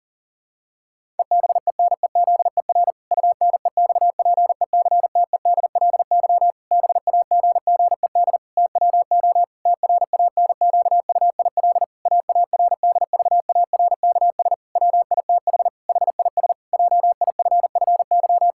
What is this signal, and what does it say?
Morse code, heard as a keyed tone, about 40 wpm, 700 Hz. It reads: EBENEZER UNEXPECTEDLY BAGGED TWO TRANQUIL AARDVARKS WITH HIS JIFFY
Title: Pangram Challenge v2 - 40wpm